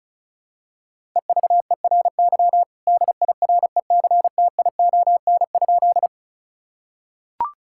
Morse code, heard as a keyed tone, about 34 wpm, 700 Hz. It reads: EVERY DIRECTION?
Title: Sentences from Top 1000 Words - Review Part 5 - 35wpm